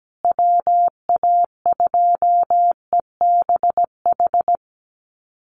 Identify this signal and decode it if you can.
Morse code, heard as a keyed tone, about 17 wpm, 700 Hz. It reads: WA2EBH
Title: USA Call Signs Repeated 2x 17wpm